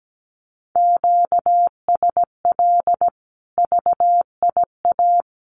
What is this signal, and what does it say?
Morse code, heard as a keyed tone, about 17 wpm, 700 Hz. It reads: QSL VIA
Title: Common QSO Phrases 17wpm